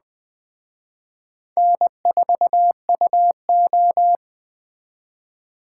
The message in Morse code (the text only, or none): N4UO